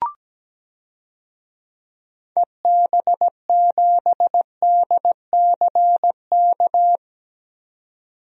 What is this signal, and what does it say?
Morse code, heard as a keyed tone, about 17 wpm, 700 Hz. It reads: EB7DCK
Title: DXCC-100 Call Signs - 6 Characters - 17wpm